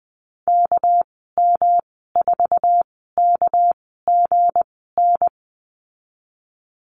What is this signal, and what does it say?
Morse code, heard as a keyed tone, about 19 wpm, spaced as for 14 wpm, 700 Hz. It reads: KM4KGN
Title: 207 Lesson 33g - Letter j - Call Signs 20wpm